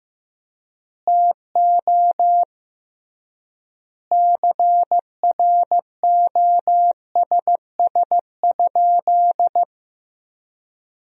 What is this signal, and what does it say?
Morse code, heard as a keyed tone, about 15 wpm, 700 Hz. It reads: TO CROSS?
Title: 3x Word-Spacing - Sentences from Top 300 Words - 15wpm